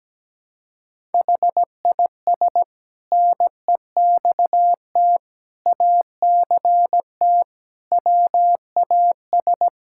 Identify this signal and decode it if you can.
Morse code, heard as a keyed tone, about 17 wpm, 700 Hz. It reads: HIS NEXT ACT WAS
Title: Sentences from Top 1000 Words - Review Part 5 - 17wpm